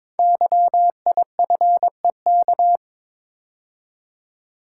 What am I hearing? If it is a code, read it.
Morse code, heard as a keyed tone, about 22 wpm, 700 Hz. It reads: YIFEK
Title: Code Groups - 5 Letters - 22wpm